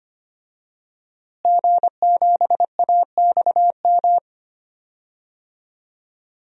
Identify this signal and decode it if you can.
Morse code, heard as a keyed tone, about 25 wpm, 700 Hz. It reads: G7AXM
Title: DXCC-100 Call Signs - 5 Characters - Repeated 2x - 25wpm